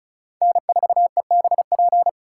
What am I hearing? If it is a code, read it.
Morse code, heard as a keyed tone, about 34 wpm, 700 Hz. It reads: N4EBP
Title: USA Call Signs - 5 Characters - 4x Character Spacing - 35wpm